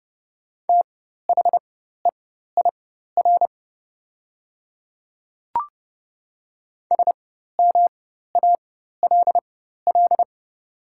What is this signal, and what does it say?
Morse code, heard as a keyed tone, about 30 wpm, spaced as for 14 wpm, 700 Hz. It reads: THEIR SMALL
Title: 4x Character Spacing - Top 300 Words - 30wpm